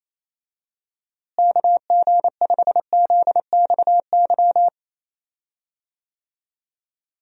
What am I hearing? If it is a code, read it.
Morse code, heard as a keyed tone, about 28 wpm, 700 Hz. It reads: KG5ZXY
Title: USA Call Signs - 6 Characters - Repeated 2x - 28wpm